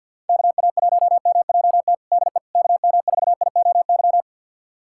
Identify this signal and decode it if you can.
Morse code, heard as a keyed tone, about 50 wpm, 700 Hz. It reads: KA1MJT DE KM4IOY
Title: USA Callsign DE Callsign - 6 Characters - 50wpm